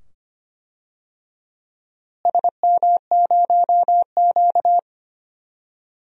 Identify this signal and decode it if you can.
Morse code, heard as a keyed tone, about 25 wpm, 700 Hz. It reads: SM0Q
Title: DXCC-100 Call Signs - 4 Characters - Repeated 2x - 25wpm